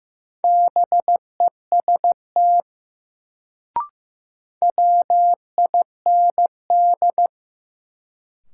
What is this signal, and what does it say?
Morse code, heard as a keyed tone, about 15 wpm, 700 Hz. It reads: BEST WIND
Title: Sentences from Top 400 Words - New Word Drill - 15wpm